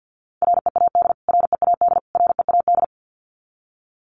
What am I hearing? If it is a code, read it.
Morse code, heard as a keyed tone, about 50 wpm, 700 Hz. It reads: READ READ READ
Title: Top 500 Words Repeated 50wpm